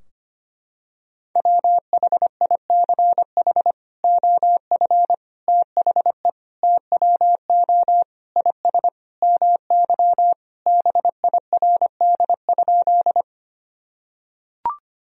Morse code, heard as a keyed tone, about 25 wpm, 700 Hz. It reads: WHICH OF THE TWO IS MY BIRD?
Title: Sentences from Top 1000 Words - Review Part 2 - 25wpm